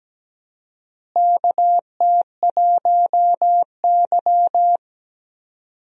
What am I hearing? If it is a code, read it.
Morse code, heard as a keyed tone, about 17 wpm, 700 Hz. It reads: KT1Y
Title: USA Call Signs - 4 Characters - Repeated 2x - 17wpm